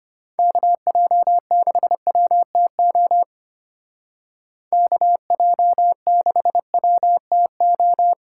Morse code, heard as a keyed tone, about 32 wpm, 700 Hz. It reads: KJ6WTO KJ6WTO